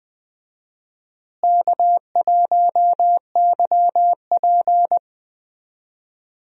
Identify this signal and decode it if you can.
Morse code, heard as a keyed tone, about 20 wpm, 700 Hz. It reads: K1YP
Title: USA Call Signs - 4 Characters - Repeated 3x - 20wpm